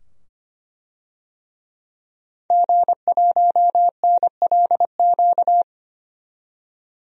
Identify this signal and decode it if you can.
Morse code, heard as a keyed tone, about 25 wpm, 700 Hz. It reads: G1NLQ
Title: DXCC-100 Call Signs - 5 Characters - 25wpm